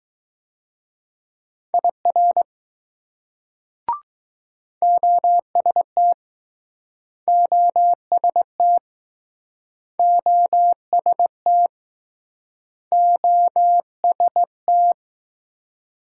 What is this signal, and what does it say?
Morse code, heard as a keyed tone, about 21 wpm, 700 Hz. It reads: IR OST OST OST OST